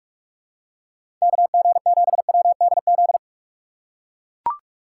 Morse code, heard as a keyed tone, about 43 wpm, 700 Hz. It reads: K G 7 W D Z
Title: USA Call Signs - 6 Characters - 6x Character Spacing - 45wpm